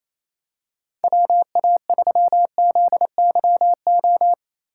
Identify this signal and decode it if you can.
Morse code, heard as a keyed tone, about 27 wpm, 700 Hz. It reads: WA3ZYO
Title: USA Call Signs - 6x Character Spacing - 28wpm